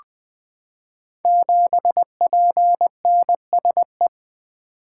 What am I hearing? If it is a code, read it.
Morse code, heard as a keyed tone, about 20 wpm, 700 Hz. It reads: 7PNSE
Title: Code Groups - 5 Letter:Numbers - 20wpm